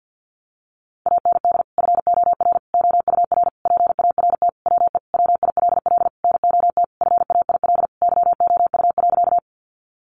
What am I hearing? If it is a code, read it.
Morse code, heard as a keyed tone, about 50 wpm, 700 Hz. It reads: AND FOR OUR PART WE WILL NOT FAIL YOU.
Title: Sentences from Top 1100 Words 50wpm